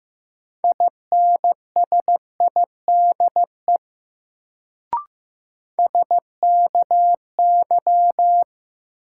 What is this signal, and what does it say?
Morse code, heard as a keyed tone, about 15 wpm, 700 Hz. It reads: INSIDE SKY
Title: Sentences from Top 1200 Words - New Word Drill Alt - 15wpm